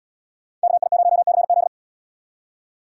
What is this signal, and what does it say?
Morse code, heard as a keyed tone, about 77 wpm, 700 Hz. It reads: KI0CZ